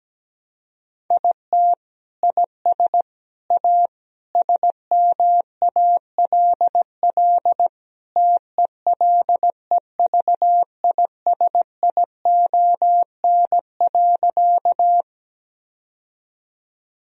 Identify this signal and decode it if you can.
Morse code, heard as a keyed tone, about 17 wpm, 700 Hz. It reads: IT IS A SMALL TELEVISION.
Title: Sentences from Top 1000 Words - Review Part 4 - 17wpm